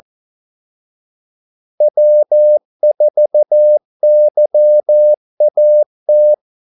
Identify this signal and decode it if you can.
Morse code, heard as a keyed tone, about 14 wpm, 600 Hz. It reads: W4YAT
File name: Callsigns-14wpm